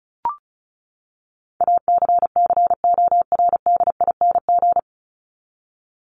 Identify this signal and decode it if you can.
Morse code, heard as a keyed tone, about 34 wpm, 700 Hz. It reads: ACCORDING
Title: LY Suffix - Top 100 Words with Base Words 35wpm